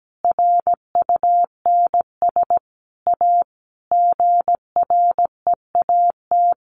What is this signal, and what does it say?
Morse code, heard as a keyed tone, about 17 wpm, 700 Hz. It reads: RUNS A GREAT
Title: Sentences from Top 600 Words 17wpm